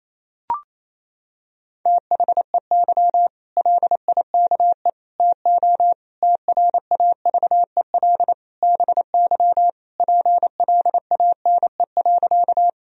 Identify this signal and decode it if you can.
Morse code, heard as a keyed tone, about 28 wpm, 700 Hz. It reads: THEY LIKE TO TRAVEL BY PLANE.
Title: Sentences from Top 1000 Words - Review Part 3 - 28wpm